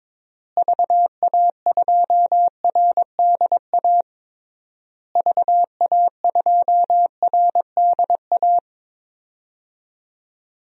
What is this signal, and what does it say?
Morse code, heard as a keyed tone, about 22 wpm, 700 Hz. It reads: VA2RDA VA2RDA
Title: DXCC-100 Call Signs - 6 Characters - Repeated 3x - 22wpm